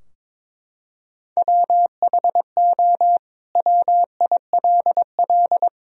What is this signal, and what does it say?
Morse code, heard as a keyed tone, about 22 wpm, 700 Hz. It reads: WHO WILL
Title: Sentences from Top 1000 Words - Review Part 2 - 22wpm